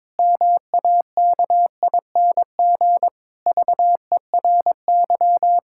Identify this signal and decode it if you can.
Morse code, heard as a keyed tone, about 22 wpm, 700 Hz. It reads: MAKING VERY
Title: Pangram Challenge v2 - 22wpm